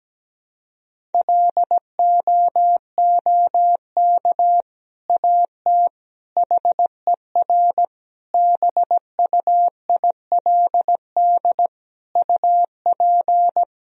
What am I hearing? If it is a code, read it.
Morse code, heard as a keyed tone, about 17 wpm, 700 Hz. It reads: LOOK AT HER BUILD UP
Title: Sentences from Top 1000 Words - Review Part 1 - 17wpm